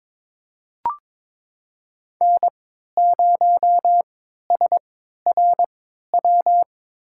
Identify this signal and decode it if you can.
Morse code, heard as a keyed tone, about 22 wpm, spaced as for 13 wpm, 700 Hz. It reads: N0SRW
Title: USA Call Signs - 3x Character Spacing - 22wpm